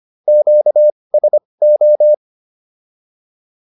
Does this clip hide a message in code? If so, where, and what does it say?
Yes: Morse code from 0.3 to 2.1 s, heard as a keyed tone, about 25 wpm, spaced as for 20 wpm, 600 Hz. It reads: QSO